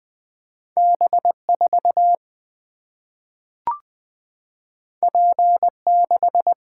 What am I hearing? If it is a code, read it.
Morse code, heard as a keyed tone, about 20 wpm, 700 Hz. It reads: B 4 P 6
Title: Single-Letter Single-Number 20wpm